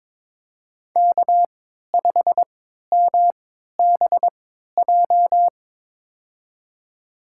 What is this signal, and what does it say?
Morse code, heard as a keyed tone, about 22 wpm, spaced as for 12 wpm, 700 Hz. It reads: K5MBJ
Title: USA Call Signs - 5 Characters - 3x Character Spacing - 22wpm